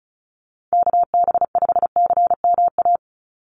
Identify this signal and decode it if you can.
Morse code, heard as a keyed tone, about 34 wpm, 700 Hz. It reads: KB5CMA
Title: USA Call Signs 35wpm